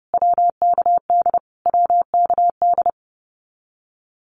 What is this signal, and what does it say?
Morse code, heard as a keyed tone, about 30 wpm, 700 Hz. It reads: WKD WKD
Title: Abbreviations Repeated 3x - 30wpm